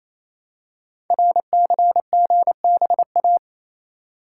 Morse code, heard as a keyed tone, about 27 wpm, 700 Hz. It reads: RCGBA
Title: Code Groups - 5 Letters - 28wpm